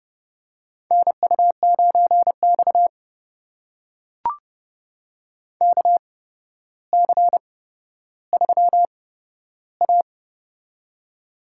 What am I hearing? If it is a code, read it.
Morse code, heard as a keyed tone, about 31 wpm, spaced as for 13 wpm, 700 Hz. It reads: NU9X KC3A
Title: USA Call Signs - 8x Character Spacing - 30wpm